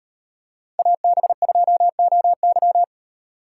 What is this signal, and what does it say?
Morse code, heard as a keyed tone, about 34 wpm, 700 Hz. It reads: AB2OY